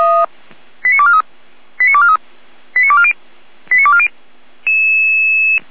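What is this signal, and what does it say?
Five-tone selective call (ZVEI), heard as quick runs of five tones, tones of about 70 ms. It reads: 89243 89243 89240 89240
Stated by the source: ZVEI five-tone call received by SDR, 455.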